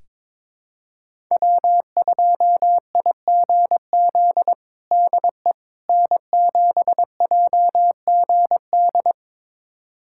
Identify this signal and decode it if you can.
Morse code, heard as a keyed tone, about 22 wpm, 700 Hz. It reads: W2IGZ DE N7JGD
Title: USA Callsign DE Callsign - 5 Characters - 22wpm